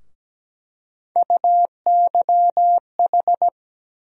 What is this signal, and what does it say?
Morse code, heard as a keyed tone, about 17 wpm, 700 Hz. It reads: UYH
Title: USA Call Sign Postfixes - 3 Letters 17wpm